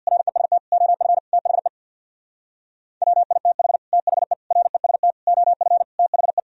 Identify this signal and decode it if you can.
Morse code, heard as a keyed tone, about 62 wpm, 700 Hz. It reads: REST OF THE WITH THE REST OF THE
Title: Speed-Racing - Sets of 5 Words 50wpm